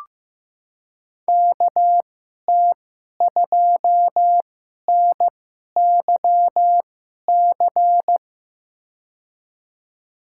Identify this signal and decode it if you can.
Morse code, heard as a keyed tone, about 16 wpm, spaced as for 11 wpm, 700 Hz. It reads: KT2NYC
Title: USA Call Signs - 6 Characters - 2x Character Spacing - 15wpm